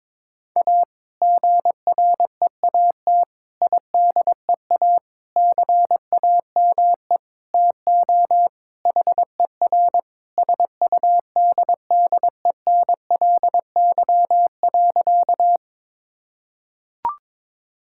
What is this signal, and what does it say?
Morse code, heard as a keyed tone, about 22 wpm, 700 Hz. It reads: A GREAT IDEA CAME TO HER SUDDENLY.
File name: practice / 22wpm / Sentences with 2-Word Phrases - 22wpm